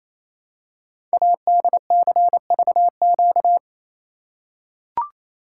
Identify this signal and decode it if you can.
Morse code, heard as a keyed tone, about 29 wpm, 700 Hz. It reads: ADCVQ
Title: Code Groups - 5 Letters - 28wpm